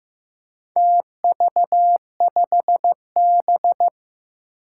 Morse code, heard as a keyed tone, about 15 wpm, 700 Hz. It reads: TV5B
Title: Four Challenging Letter-Number Combinations 15wpm